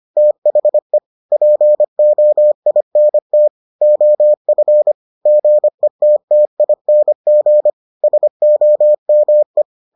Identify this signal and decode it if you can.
Morse code, heard as a keyed tone, about 25 wpm, 600 Hz. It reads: THE POINT OF GETTING SOME